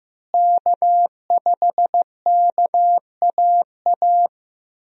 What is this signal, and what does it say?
Morse code, heard as a keyed tone, about 14 wpm, 700 Hz. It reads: K5KAA
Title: USA Call Signs - 5 Characters - Repeated 2x - 15wpm